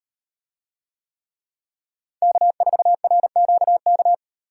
Morse code, heard as a keyed tone, about 37 wpm, 700 Hz. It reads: K4RQK